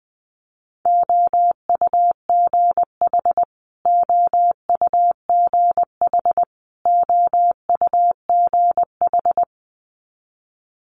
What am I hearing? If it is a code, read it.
Morse code, heard as a keyed tone, about 20 wpm, 700 Hz. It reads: OUGH OUGH OUGH
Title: Multi-letter Phonemes Repeated 3x 20wpm